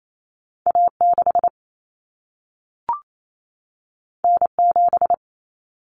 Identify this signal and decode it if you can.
Morse code, heard as a keyed tone, about 28 wpm, 700 Hz. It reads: A6 N7
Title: Common Cut Numbers - 2-Digit 28wpm